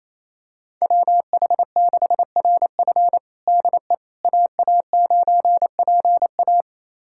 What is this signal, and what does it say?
Morse code, heard as a keyed tone, about 27 wpm, 700 Hz. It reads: WH6RF DE AA9PA
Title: USA Callsign DE Callsign - 5 Characters - 28wpm